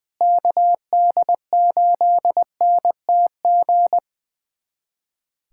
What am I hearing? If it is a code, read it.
Morse code, heard as a keyed tone, about 21 wpm, 700 Hz. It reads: KD8NTG
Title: USA Callsign DE Callsign - 6 Characters - 20wpm